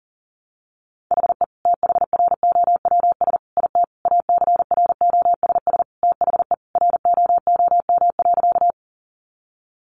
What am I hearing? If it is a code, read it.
Morse code, heard as a keyed tone, about 40 wpm, 700 Hz. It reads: HE THROWS IT ACROSS THE ROOM.
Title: Additional Easy Phrases 40wpm